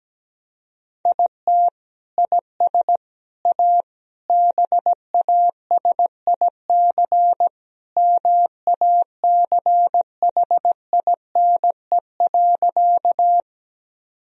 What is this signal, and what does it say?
Morse code, heard as a keyed tone, about 17 wpm, 700 Hz. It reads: IT IS A BASIC MACHINE.
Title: Sentences from Top 1000 Words - Review Part 3 - 17wpm